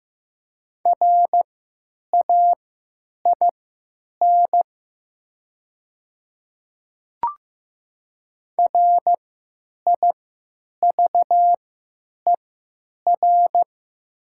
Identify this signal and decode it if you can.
Morse code, heard as a keyed tone, about 15 wpm, spaced as for 9 wpm, 700 Hz. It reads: RAIN RIVER
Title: 3x Character Spacing - Top 300 Words - 15wpm